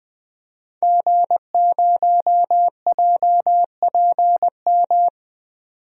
Morse code, heard as a keyed tone, about 20 wpm, 700 Hz. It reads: G0JPM
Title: DXCC-100 Call Signs - 5 Characters - 20wpm